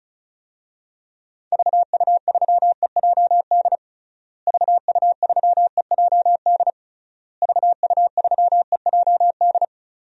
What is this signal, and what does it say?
Morse code, heard as a keyed tone, about 34 wpm, 700 Hz. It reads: VU3EJD VU3EJD VU3EJD
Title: DXCC-100 Call Signs - 6 Characters - Repeated 3x - 35wpm